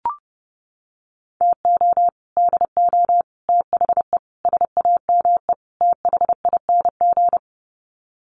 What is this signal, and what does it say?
Morse code, heard as a keyed tone, about 30 wpm, 700 Hz. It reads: TO DO THE SAME THING
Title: Sets of 5 Words Encore 30wpm